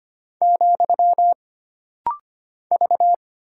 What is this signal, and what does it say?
Morse code, heard as a keyed tone, about 25 wpm, 700 Hz. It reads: , V
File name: Single Character (Letter:Number:Punctuation:Prosign) - Rapid-Fire - 25wpm